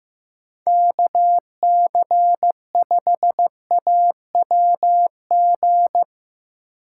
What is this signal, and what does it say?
Morse code, heard as a keyed tone, about 17 wpm, 700 Hz. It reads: KC5AWG
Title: Speed Racing - USA Call Signs 15wpm